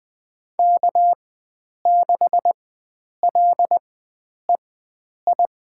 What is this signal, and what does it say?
Morse code, heard as a keyed tone, about 20 wpm, spaced as for 9 wpm, 700 Hz. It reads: K6LEI
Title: USA Call Signs - 5 Characters - 4x Character Spacing - 20wpm